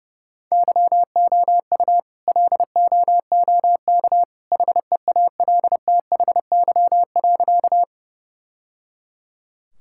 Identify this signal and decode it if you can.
Morse code, heard as a keyed tone, about 31 wpm, 700 Hz. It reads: YOU LOOK HEALTHY.
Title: Sentences from Top 1000 Words - Review Part 5 - 30wpm